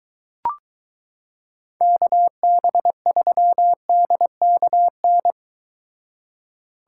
Morse code, heard as a keyed tone, about 24 wpm, 700 Hz. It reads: KB3DKN